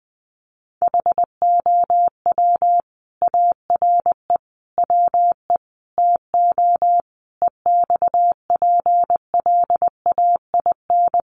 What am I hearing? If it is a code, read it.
Morse code, heard as a keyed tone, about 20 wpm, 700 Hz. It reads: HOW ARE WE TO EXPLAIN
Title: Sentences from Top 1100 Words 20wpm